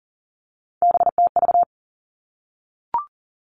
Morse code, heard as a keyed tone, about 40 wpm, 700 Hz. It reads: BTV
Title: Three Challenging Letter-Number Combinations 40wpm